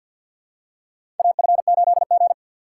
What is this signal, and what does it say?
Morse code, heard as a keyed tone, about 53 wpm, 700 Hz. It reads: AF8G
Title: USA Call Signs - 4 Characters - 3x Character Spacing - 50wpm